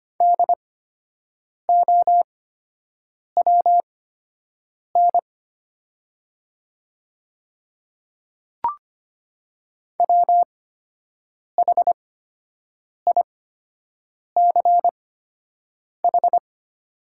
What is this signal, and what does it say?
Morse code, heard as a keyed tone, about 25 wpm, spaced as for 7 wpm, 700 Hz. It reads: DOWN WHICH
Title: 8x Character Spacing - Top 100 Words - 25wpm